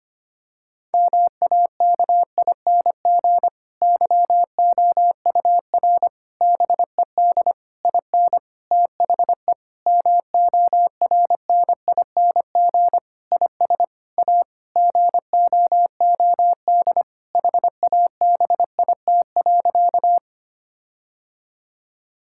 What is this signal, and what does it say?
Morse code, heard as a keyed tone, about 25 wpm, 700 Hz. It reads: MAKING YOUR BED IN THE MORNING IS A GOOD HABIT.
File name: practice / 25wpm / Difficult Sentences with Hints - 25wpm